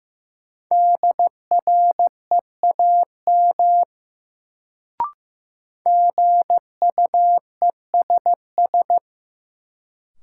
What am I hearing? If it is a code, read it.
Morse code, heard as a keyed tone, about 15 wpm, 700 Hz. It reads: DREAM GUESS
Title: Sentences from Top 1500 Words - New Word Drill - 15wpm